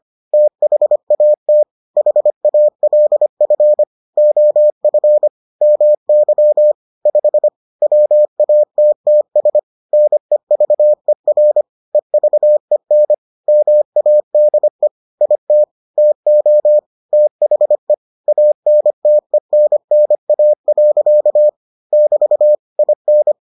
Morse code, heard as a keyed tone, about 25 wpm, 600 Hz. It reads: THAT HALF OF MY 5 WATTS NEVER EVEN MADE IT TO THE ANTENNA. <BT> IN